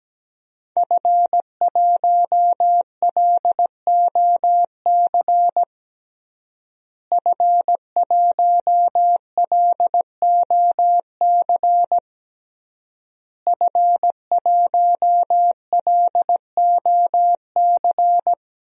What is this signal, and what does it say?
Morse code, heard as a keyed tone, about 17 wpm, 700 Hz. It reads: F1LOC F1LOC F1LOC
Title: DXCC-100 Call Signs - 5 Characters - Repeated 3x - 17wpm